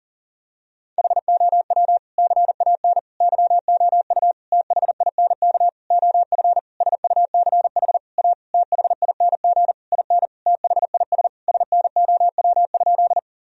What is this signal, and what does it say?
Morse code, heard as a keyed tone, about 40 wpm, 700 Hz. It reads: HOW CAN YOU THINK OF SUCH A THING IN THIS SNOW?
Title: Sentences from Top 1000 Words - Review Part 3 - 40wpm